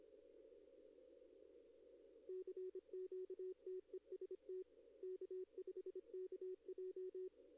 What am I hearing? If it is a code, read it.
Morse code, heard as a keyed tone, about 26 wpm, 370 Hz. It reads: CQ TEST K5KJ
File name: contests / arrl-160-k5tr-op-w5tn -1300